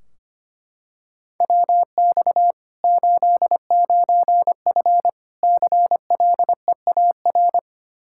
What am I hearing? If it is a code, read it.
Morse code, heard as a keyed tone, about 25 wpm, 700 Hz. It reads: WX 89F CLEAR
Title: Practice QSO - Exchange 2 - Info - 25wpm